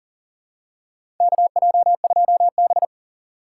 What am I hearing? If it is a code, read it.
Morse code, heard as a keyed tone, about 40 wpm, 700 Hz. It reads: KJ2B
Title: USA Call Signs - 4 Characters - 4x Character Spacing - 40wpm